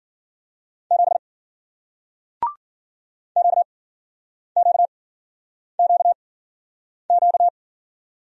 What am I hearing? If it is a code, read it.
Morse code, heard as a keyed tone, about 58 wpm, spaced as for 9 wpm, 700 Hz. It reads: 7 QQQQ